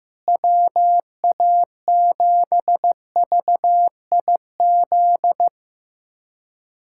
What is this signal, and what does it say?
Morse code, heard as a keyed tone, about 15 wpm, 700 Hz. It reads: WA7VIZ
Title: USA Callsign DE Callsign - 6 Characters - 15wpm